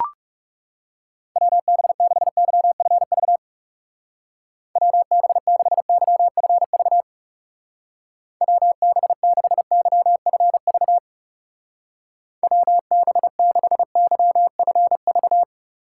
Morse code, heard as a keyed tone, about 36 wpm, 700 Hz. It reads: WB6YFV WB6YFV WB6YFV WB6YFV
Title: Speed Racing - USA 2x3 Call Signs 30wpm